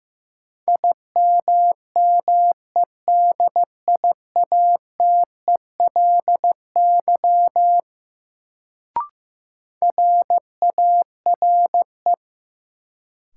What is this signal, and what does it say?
Morse code, heard as a keyed tone, about 15 wpm, 700 Hz. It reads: IMMEDIATELY RARE
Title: Sentences from Top 1100 Words - New Word Drill - 15wpm